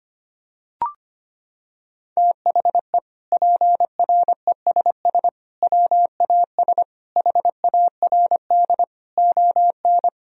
Morse code, heard as a keyed tone, about 25 wpm, 700 Hz. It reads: THE PRESS WAS HARD ON
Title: Sentences from Top 1000 Words - Review Part 1 - 25wpm